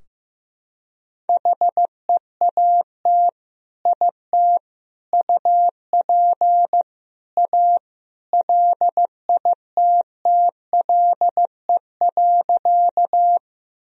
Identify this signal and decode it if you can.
Morse code, heard as a keyed tone, about 15 wpm, 700 Hz. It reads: HEAT IT UP A LITTLE.
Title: Sentences from Top 1000 Words - Review Part 3 - 15wpm